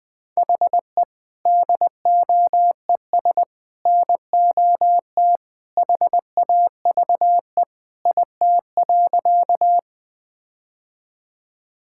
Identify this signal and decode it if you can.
Morse code, heard as a keyed tone, about 20 wpm, 700 Hz. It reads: HE DOES NOT HAVE IT.
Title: Sentences with 3-Word Phrases - 20wpm